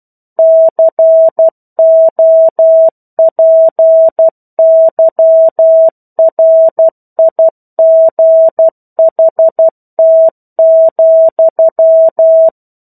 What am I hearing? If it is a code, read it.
Morse code, heard as a keyed tone, about 12 wpm, 650 Hz. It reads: COPYRIGHT,
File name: 01-Chapter-01 12 WPM